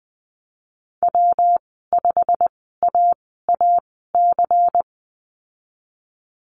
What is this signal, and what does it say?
Morse code, heard as a keyed tone, about 20 wpm, spaced as for 15 wpm, 700 Hz. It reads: W5AAC
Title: 154 Lesson 25g - Letter p - Call Signs 20wpm